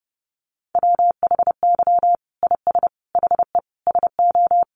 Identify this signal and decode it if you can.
Morse code, heard as a keyed tone, about 30 wpm, 700 Hz. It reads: WHY IS HE SO
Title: Sentences from Top 1200 Words 30wpm